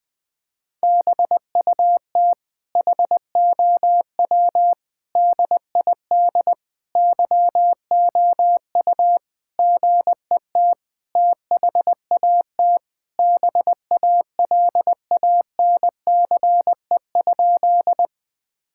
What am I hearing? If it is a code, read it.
Morse code, heard as a keyed tone, about 20 wpm, 700 Hz. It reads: BUT HOW DID YOU GET THAT BALANCE?
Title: Sentences from Top 1000 Words - Review Part 6 - 20wpm